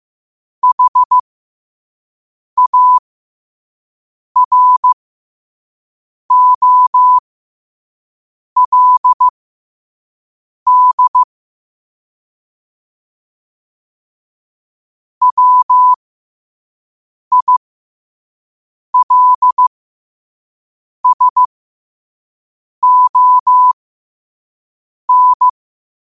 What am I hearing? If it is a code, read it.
Morse code, heard as a keyed tone, about 15 wpm, spaced as for 5 wpm, 1000 Hz. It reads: HAROLD WILSON
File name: qotdfiles / 5wpm